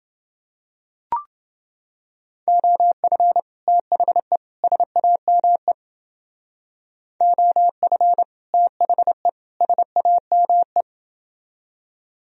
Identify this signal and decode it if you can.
Morse code, heard as a keyed tone, about 29 wpm, 700 Hz. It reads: OF THE SAME OF THE SAME